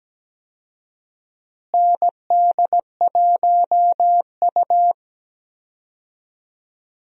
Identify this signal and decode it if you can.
Morse code, heard as a keyed tone, about 17 wpm, 700 Hz. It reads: ND1U
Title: USA Call Signs - 4 Characters - Repeated 2x - 17wpm